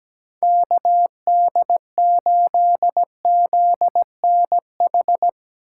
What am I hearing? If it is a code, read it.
Morse code, heard as a keyed tone, about 20 wpm, 700 Hz. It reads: KD8ZNH
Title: Speed Racing - USA 2x3 Call Signs 17wpm